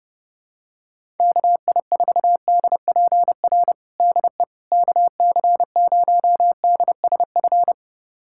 Morse code, heard as a keyed tone, about 30 wpm, 700 Hz. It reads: KI4DPR DE KC0DSF
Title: USA Callsign DE Callsign 30wpm